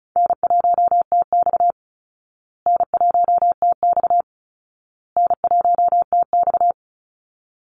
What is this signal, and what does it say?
Morse code, heard as a keyed tone, about 34 wpm, 700 Hz. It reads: N1TX N1TX N1TX
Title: Easy Call Signs Repeated 3x - 35wpm